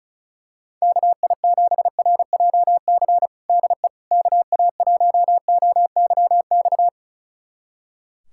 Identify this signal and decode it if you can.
Morse code, heard as a keyed tone, about 34 wpm, 700 Hz. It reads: KI7RJC DE KA1OYX
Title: USA Callsign DE Callsign - 6 Characters - 35wpm